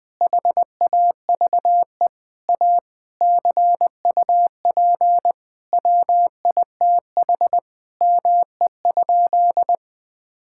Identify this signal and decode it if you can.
Morse code, heard as a keyed tone, about 21 wpm, 700 Hz. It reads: HAVE A CUP WITH ME?
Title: Sentences from Top 1600 Words 20wpm